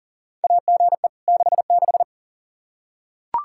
Morse code, heard as a keyed tone, about 40 wpm, 700 Hz. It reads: AGE 66
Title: Practice QSO - Exchange 3 - More Info - 40wpm